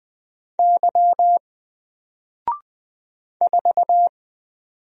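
Morse code, heard as a keyed tone, about 20 wpm, 700 Hz. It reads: Y 4
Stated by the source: Single letter and number Morse drill at 20 wpm, from N4LCD.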